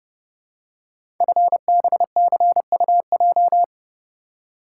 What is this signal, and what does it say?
Morse code, heard as a keyed tone, about 30 wpm, 700 Hz. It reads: FBCUJ
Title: Code Groups - 5 Letters - 30wpm